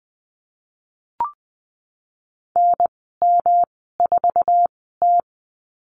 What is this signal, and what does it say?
Morse code, heard as a keyed tone, about 20 wpm, spaced as for 15 wpm, 700 Hz. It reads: NM4T
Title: 147 Lesson 24g - Letter y - Call Signs 20wpm